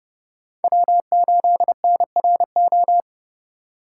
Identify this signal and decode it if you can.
Morse code, heard as a keyed tone, about 38 wpm, 700 Hz. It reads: W8NRO